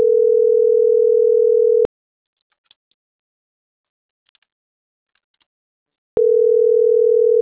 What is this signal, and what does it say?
Ringback tone (North America), 440 + 480 Hz.